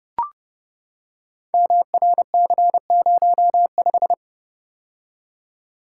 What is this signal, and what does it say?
Morse code, heard as a keyed tone, about 28 wpm, 700 Hz. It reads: MRC05
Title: Code Groups - 5 Letter:Numbers - 30wpm